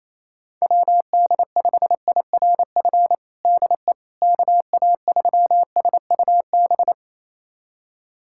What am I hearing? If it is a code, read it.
Morse code, heard as a keyed tone, about 28 wpm, 700 Hz. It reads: WD5IRF DE KA3SUB
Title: USA Callsign DE Callsign - 6 Characters - 28wpm